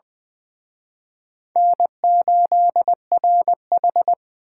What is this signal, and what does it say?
Morse code, heard as a keyed tone, about 24 wpm, 700 Hz. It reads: N8RH
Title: Speed Racing - USA Call Signs 20wpm